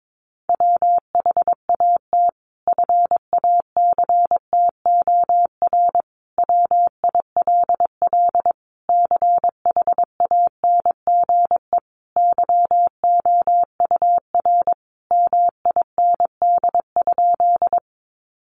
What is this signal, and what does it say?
Morse code, heard as a keyed tone, about 22 wpm, 700 Hz. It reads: WHAT FACTOR WILL CHANGE YOUR MIND?
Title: Sentences from Top 1300 Words 22wpm